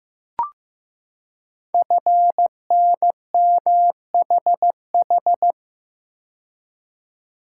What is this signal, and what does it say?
Morse code, heard as a keyed tone, about 15 wpm, 700 Hz. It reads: FNMHH
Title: Code Groups - 5 Letters - 15wpm